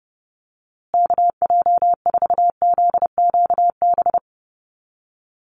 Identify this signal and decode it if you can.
Morse code, heard as a keyed tone, about 29 wpm, 700 Hz. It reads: KJ4ZQB
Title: USA Call Signs Repeated 2x 30wpm